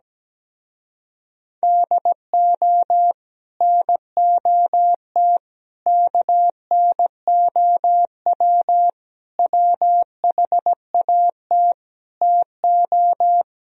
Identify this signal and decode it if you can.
Morse code, heard as a keyed tone, about 17 wpm, 700 Hz. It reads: DO NOT KNOW WHAT TO
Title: Speed-Racing - Sets of 5 Words 17wpm